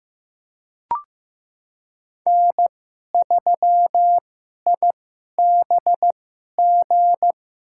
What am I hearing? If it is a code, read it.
Morse code, heard as a keyed tone, about 15 wpm, spaced as for 11 wpm, 700 Hz. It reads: N3IBG
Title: USA Call Signs - 5 Characters - 2x Character Spacing - 15wpm